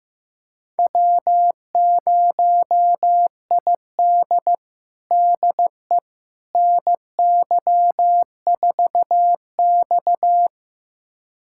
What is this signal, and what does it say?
Morse code, heard as a keyed tone, about 15 wpm, 700 Hz. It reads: W0ID DE NY4X
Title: USA Callsign DE Callsign - 4 Characters - 15wpm